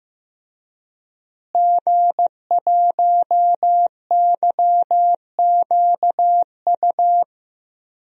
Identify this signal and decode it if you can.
Morse code, heard as a keyed tone, about 15 wpm, 700 Hz. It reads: G1YQU
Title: DXCC-100 Call Signs - 5 Characters - Repeated 2x - 15wpm